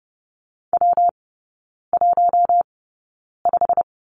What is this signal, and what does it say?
Morse code, heard as a keyed tone, about 30 wpm, spaced as for 9 wpm, 700 Hz. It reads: W15
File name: Lesson 19e - Letter w - Three Characters 30wpm